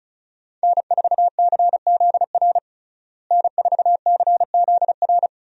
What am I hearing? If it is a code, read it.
Morse code, heard as a keyed tone, about 36 wpm, 700 Hz. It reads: N4CZR N4CZR
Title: USA Call Signs - 5 Characters - Repeated 2x - 35wpm